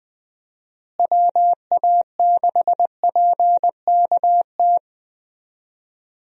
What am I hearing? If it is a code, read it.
Morse code, heard as a keyed tone, about 21 wpm, 700 Hz. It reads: WA6PKT
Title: USA Call Signs - 6 Characters - Repeated 2x - 20wpm